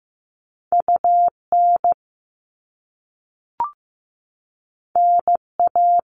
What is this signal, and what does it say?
Morse code, heard as a keyed tone, about 15 wpm, 700 Hz. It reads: UN NA
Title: All Cut Numbers - 2-Digit 15wpm